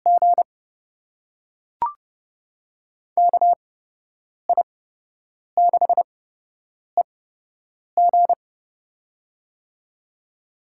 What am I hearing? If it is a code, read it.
Morse code, heard as a keyed tone, about 31 wpm, spaced as for 8 wpm, 700 Hz. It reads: G KI6EG
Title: USA Call Signs - 8x Character Spacing - 30wpm